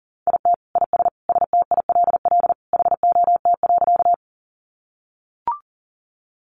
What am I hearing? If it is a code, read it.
Morse code, heard as a keyed tone, about 40 wpm, 700 Hz. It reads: IT IS STILL HOT.
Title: Easy Phrases 40wpm